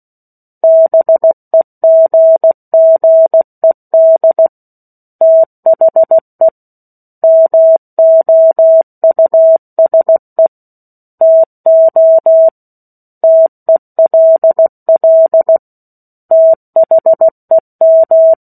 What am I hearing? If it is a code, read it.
Morse code, heard as a keyed tone, about 16 wpm, 650 Hz. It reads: BEGGED THE MOUSE TO TELL THEM
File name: 03-Chapter-03 16 WPM